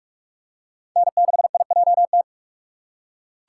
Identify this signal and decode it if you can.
Morse code, heard as a keyed tone, about 47 wpm, 700 Hz. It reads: N6IJT
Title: USA Call Signs - 5 Characters - 3x Character Spacing - 45wpm